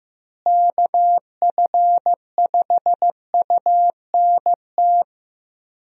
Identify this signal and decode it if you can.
Morse code, heard as a keyed tone, about 15 wpm, 700 Hz. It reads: KF5UNT
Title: USA Call Signs - 6 Characters - Repeated 3x - 15wpm